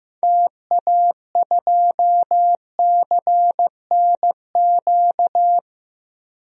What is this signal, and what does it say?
Morse code, heard as a keyed tone, about 16 wpm, 700 Hz. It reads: TA2CNQ
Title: DXCC-100 Call Signs - 6 Characters - 15wpm